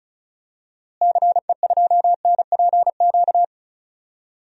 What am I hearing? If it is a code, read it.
Morse code, heard as a keyed tone, about 36 wpm, 700 Hz. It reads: CE2NPQ
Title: DXCC-100 Call Signs - 6 Characters - Repeated 3x - 35wpm